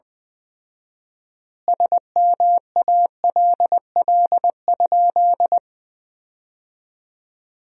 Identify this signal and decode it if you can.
Morse code, heard as a keyed tone, about 19 wpm, 700 Hz. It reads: SMALL?
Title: 4x Word-Spacing - Sentences from Top 200 Words - 20wpm